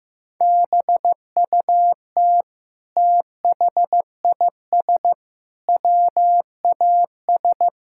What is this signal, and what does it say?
Morse code, heard as a keyed tone, about 15 wpm, 700 Hz. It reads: BUT THIS WAS
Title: Sentences from Top 1000 Words - Review Part 6 - 15wpm